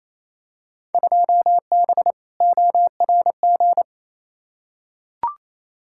Morse code, heard as a keyed tone, about 28 wpm, 700 Hz. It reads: 2B ORG
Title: Field Day Exchanges Encore - 28wpm